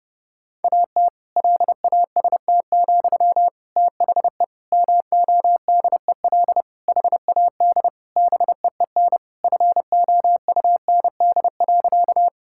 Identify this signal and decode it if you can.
Morse code, heard as a keyed tone, about 31 wpm, 700 Hz. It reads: AT LAST, THE MODEL HAD BEEN FOUND.
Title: Sentences from Top 1000 Words - Review Part 5 - 30wpm